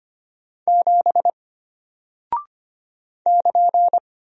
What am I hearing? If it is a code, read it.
Morse code, heard as a keyed tone, about 25 wpm, 700 Hz. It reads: 7 <KN>
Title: Single Character (Letter:Number:Punctuation:Prosign) Alt 25wpm